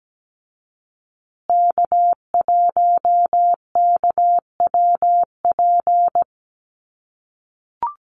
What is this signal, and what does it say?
Morse code, heard as a keyed tone, about 17 wpm, 700 Hz. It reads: K1KWP
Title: USA Call Signs Repeated 3x 17wpm